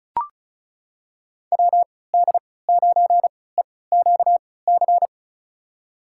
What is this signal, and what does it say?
Morse code, heard as a keyed tone, about 36 wpm, spaced as for 20 wpm, 700 Hz. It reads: WD9EQC
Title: USA Call Signs - 6 Characters - 3x Character Spacing - 35wpm